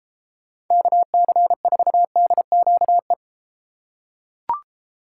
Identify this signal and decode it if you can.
Morse code, heard as a keyed tone, about 32 wpm, 700 Hz. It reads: KC4DQE